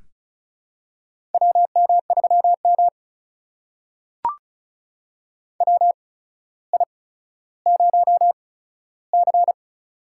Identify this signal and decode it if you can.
Morse code, heard as a keyed tone, about 36 wpm, spaced as for 15 wpm, 700 Hz. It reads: WM3M WI0C
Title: USA Call Signs - 4 Characters - 8x Character Spacing - 35wpm